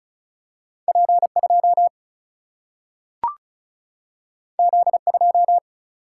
Morse code, heard as a keyed tone, about 36 wpm, 700 Hz. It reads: P2 Z2
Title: Single-Letter Single-Number 35wpm